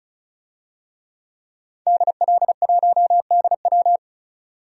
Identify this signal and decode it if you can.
Morse code, heard as a keyed tone, about 34 wpm, 700 Hz. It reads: DL1DW
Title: DXCC-100 Call Signs - 5 Characters - 35wpm